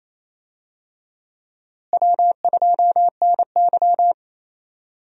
Morse code, heard as a keyed tone, about 28 wpm, 700 Hz. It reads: W2NY
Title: USA Call Signs - 4 Characters - Repeated 3x - 28wpm